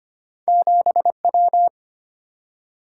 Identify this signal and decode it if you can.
Morse code, heard as a keyed tone, about 25 wpm, 700 Hz. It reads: 7W
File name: Two Letter-Number Combinations - 25wpm